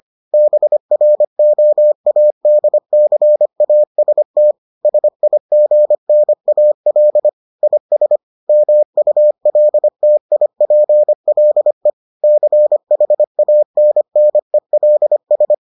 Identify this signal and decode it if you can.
Morse code, heard as a keyed tone, about 25 wpm, 600 Hz. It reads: BROADCAST SIGNAL IS MULTIPLE CHANNELS